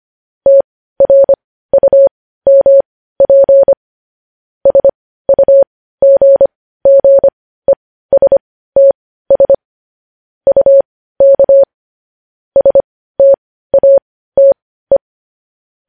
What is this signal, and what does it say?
Morse code, heard as a keyed tone, about 25 wpm, spaced as for 15 wpm, 550 Hz. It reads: TRUMP SUGGESTS UK STATE
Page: news-headlines-in-morse-code-at-15-wpm